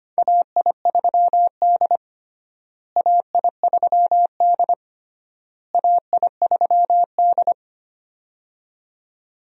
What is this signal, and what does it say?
Morse code, heard as a keyed tone, about 25 wpm, 700 Hz. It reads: AI3D AI3D AI3D
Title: USA Call Signs - 4 Characters - Repeated 3x - 25wpm